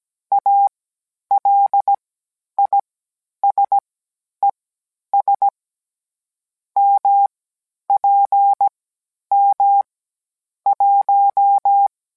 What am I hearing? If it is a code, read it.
Morse code, heard as a keyed tone, about 17 wpm, spaced as for 10 wpm, 800 Hz. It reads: ALISES MPM 1